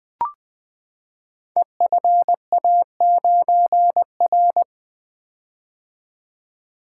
Morse code, heard as a keyed tone, about 20 wpm, 700 Hz. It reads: EFA9R
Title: Code Groups - 5 Letter:Numbers - 20wpm